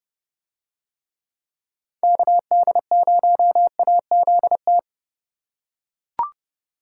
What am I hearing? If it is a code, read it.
Morse code, heard as a keyed tone, about 30 wpm, 700 Hz. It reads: KD0AZT